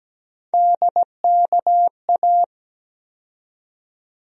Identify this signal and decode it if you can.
Morse code, heard as a keyed tone, about 17 wpm, 700 Hz. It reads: DKA
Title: USA Call Sign Postfixes - 3 Letters 17wpm